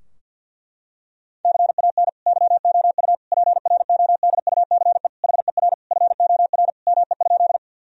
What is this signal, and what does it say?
Morse code, heard as a keyed tone, about 50 wpm, 700 Hz. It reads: CAN YOU PRODUCE HER FOR ME?
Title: Sentences from Top 1000 Words - Review Part 3 - 50wpm